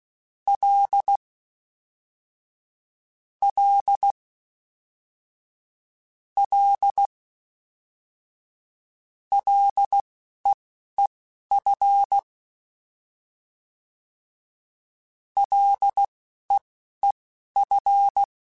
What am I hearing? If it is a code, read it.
Morse code, heard as a keyed tone, about 16 wpm, spaced as for 11 wpm, 800 Hz. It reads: L L L LEEF LEEF
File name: Short Runs / Short Runs-mp3s / SinglePrefixes @10